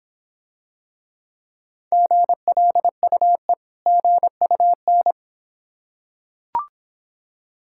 Morse code, heard as a keyed tone, about 26 wpm, 700 Hz. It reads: GLUE GUN